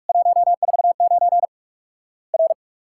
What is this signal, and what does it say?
Morse code, heard as a keyed tone, about 46 wpm, 700 Hz. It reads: 149 R
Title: Worked All Europe DX Contest - QTC Exchange - 45wpm